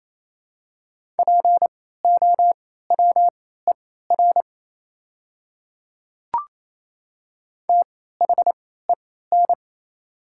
Morse code, heard as a keyed tone, about 28 wpm, spaced as for 16 wpm, 690 Hz. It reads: POWER THEN
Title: 3x Character Spacing - Top 500 Words - 28wpm